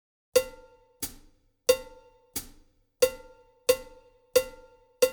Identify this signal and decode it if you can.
There is no code.